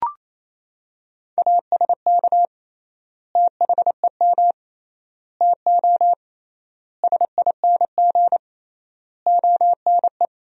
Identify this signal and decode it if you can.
Morse code, heard as a keyed tone, about 28 wpm, 700 Hz. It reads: ASK THEM TO SING ONE
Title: 3x Word-Spacing - Sentences from Top 400 Words - 28wpm